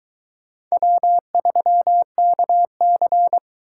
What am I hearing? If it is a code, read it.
Morse code, heard as a keyed tone, about 23 wpm, 700 Hz. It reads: W3KC